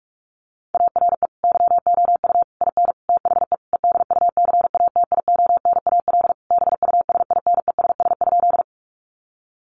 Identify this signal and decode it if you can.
Morse code, heard as a keyed tone, about 44 wpm, 700 Hz. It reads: ARE YOU IN THE EDUCATIONAL BUSINESS?
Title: Sentences from Top 1200 Words 45wpm